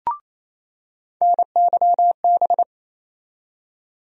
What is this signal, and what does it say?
Morse code, heard as a keyed tone, about 28 wpm, 700 Hz. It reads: NYB
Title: USA Call Sign Postfixes - 3 Letters 28wpm